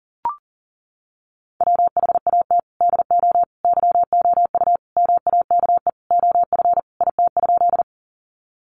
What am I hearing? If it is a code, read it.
Morse code, heard as a keyed tone, about 37 wpm, 700 Hz. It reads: WHAT DO YOU MAKE OF IT?
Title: Sentences from Top 100 Words 40wpm